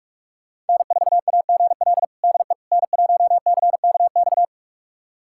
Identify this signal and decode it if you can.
Morse code, heard as a keyed tone, about 46 wpm, 700 Hz. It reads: N4AGL DE N1CKX
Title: USA Callsign DE Callsign - 5 Characters - 45wpm